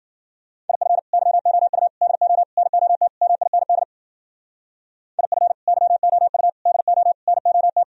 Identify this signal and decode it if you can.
Morse code, heard as a keyed tone, about 52 wpm, 700 Hz. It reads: IF YOU DO NOT MIND IF YOU DO NOT